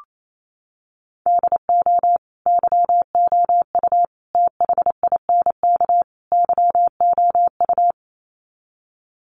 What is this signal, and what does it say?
Morse code, heard as a keyed tone, about 28 wpm, 700 Hz. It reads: DO YOU THINK YOU
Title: Sets of 4 Words 28wpm